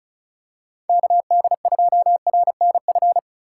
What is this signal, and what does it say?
Morse code, heard as a keyed tone, about 36 wpm, 700 Hz. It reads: KD2RNF
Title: USA Call Signs - 6 Characters - 3x Character Spacing - 35wpm